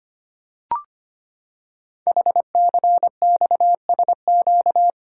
Morse code, heard as a keyed tone, about 25 wpm, 700 Hz. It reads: HCXSQ
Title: Code Groups - 5 Letters - 25wpm